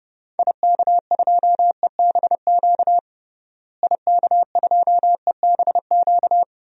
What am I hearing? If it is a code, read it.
Morse code, heard as a keyed tone, about 30 wpm, 700 Hz. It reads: IK2EBQ IK2EBQ
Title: DXCC-100 Call Signs - 6 Characters - Repeated 3x - 30wpm